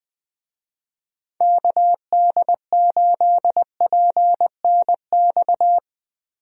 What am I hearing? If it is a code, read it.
Morse code, heard as a keyed tone, about 20 wpm, 700 Hz. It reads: KD8PNX
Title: USA Call Signs - 6 Characters - Repeated 3x - 20wpm